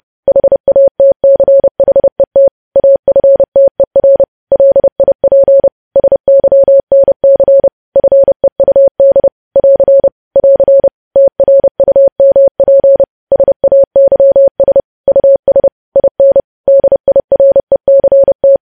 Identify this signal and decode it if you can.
Morse code, heard as a keyed tone, about 30 wpm, 550 Hz. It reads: HATCHET AFTER LIP SYNC FEUD <AR> <AR> TRUMP SAYS US IN DIRECT
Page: news headlines in morse code at 30 wpm